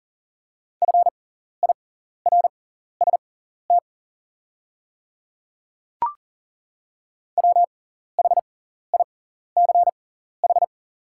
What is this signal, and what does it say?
Morse code, heard as a keyed tone, about 40 wpm, spaced as for 14 wpm, 690 Hz. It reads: FIRST WHICH
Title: 6x Character Spacing - Top 100 Words - 40wpm